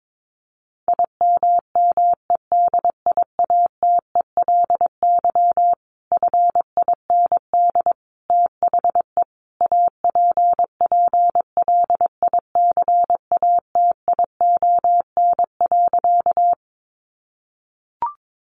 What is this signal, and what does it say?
Morse code, heard as a keyed tone, about 22 wpm, 700 Hz. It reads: IMMEDIATELY FIND THE APPLICATION.
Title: Sentences from Top 1200 Words 22wpm